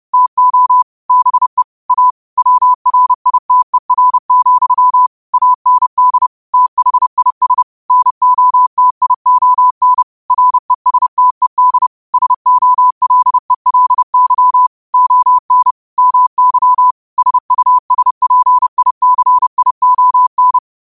Morse code, heard as a keyed tone, about 30 wpm, 1000 Hz. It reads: TO BE A WRITER, AND THIS NOTION RESTED SOLELY ON MY SUSPICION
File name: qotdfiles / 30wpm